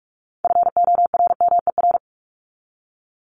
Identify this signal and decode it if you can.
Morse code, heard as a keyed tone, about 44 wpm, 700 Hz. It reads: FORMER
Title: LY Suffix - Top 100 Words with Base Words 45wpm